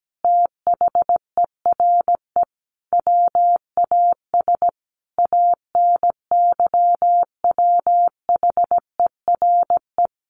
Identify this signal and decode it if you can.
Morse code, heard as a keyed tone, about 17 wpm, 700 Hz. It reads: THERE WAS ANYWHERE
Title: Sentences from Top 1400 Words 17wpm